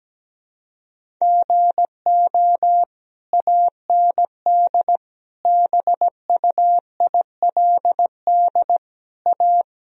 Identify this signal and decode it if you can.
Morse code, heard as a keyed tone, about 17 wpm, 700 Hz. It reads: GO AND BUILD A
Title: Sentences from Top 1000 Words - Review Part 1 - 17wpm